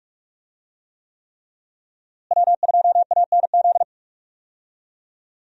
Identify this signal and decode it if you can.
Morse code, heard as a keyed tone, about 42 wpm, 700 Hz. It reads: W2ANZ
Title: USA Call Signs - 5 Characters - Repeated 3x - 45wpm